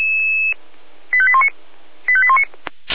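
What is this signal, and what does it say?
Five-tone selective call (ZVEI), heard as quick runs of five tones, tones of about 70 ms. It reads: 85819 85819
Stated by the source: Five-tone ZVEI selective call received by SDR, from the feuerwehr-orlamuende.org fire brigade audio server.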